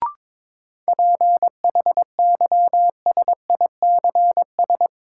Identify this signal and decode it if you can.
Morse code, heard as a keyed tone, about 22 wpm, 700 Hz. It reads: PHYSICS
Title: Sentences from Top 1500 Words - New Word Drill Alt - 22wpm